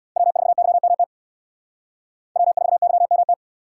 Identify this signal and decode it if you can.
Morse code, heard as a keyed tone, about 78 wpm, 700 Hz. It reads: W4YMT W4YMT